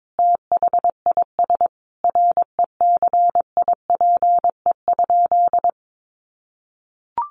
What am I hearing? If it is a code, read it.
Morse code, heard as a keyed tone, about 22 wpm, 700 Hz. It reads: THIS RECIPE?
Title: Sentences from Top 1200 Words 22wpm